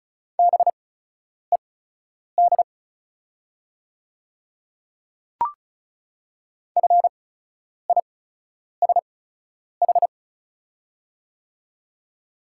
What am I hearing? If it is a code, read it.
Morse code, heard as a keyed tone, about 36 wpm, spaced as for 10 wpm, 700 Hz. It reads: BED FISH
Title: 8x Character Spacing - Top 500 Words - 35wpm